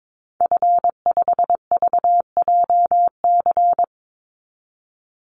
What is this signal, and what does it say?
Morse code, heard as a keyed tone, about 22 wpm, 700 Hz. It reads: F5VJC
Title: Call Signs Repeated 3x - 22wpm